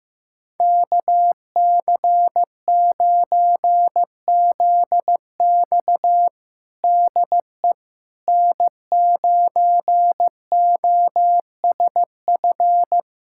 Morse code, heard as a keyed tone, about 15 wpm, 700 Hz. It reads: KC9ZX DE N9OSF
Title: USA Callsign DE Callsign - 5 Characters - 15wpm